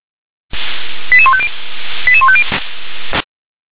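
Five-tone selective call (ZVEI), heard as quick runs of five tones, tones of about 70 ms. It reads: 88150 88150
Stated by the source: ZVEI five-tone selective call received on SDR.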